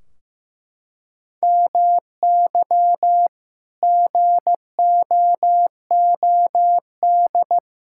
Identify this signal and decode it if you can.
Morse code, heard as a keyed tone, about 15 wpm, 700 Hz. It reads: MY GOOD
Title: Sentences from Top 1600 Words 15wpm